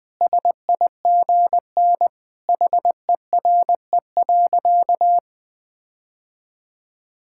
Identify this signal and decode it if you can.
Morse code, heard as a keyed tone, about 21 wpm, 700 Hz. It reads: SIGN HERE.
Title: Sentences with 6-Word Phrases - 20wpm